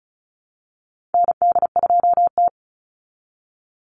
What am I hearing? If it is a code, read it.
Morse code, heard as a keyed tone, about 34 wpm, 700 Hz. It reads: ND2T
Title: Call Signs Repeated 2x - 35wpm